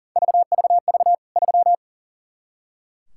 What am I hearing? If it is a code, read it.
Morse code, heard as a keyed tone, about 40 wpm, 700 Hz. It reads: VVV 3
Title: K1USN SST QSO Practice 40wpm